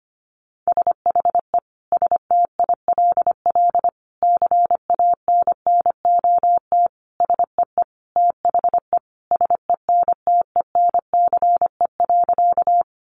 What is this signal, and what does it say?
Morse code, heard as a keyed tone, about 25 wpm, 700 Hz. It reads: SHE STILL CANNOT SEE THE SENTENCE.
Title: Additional Easy Phrases 25wpm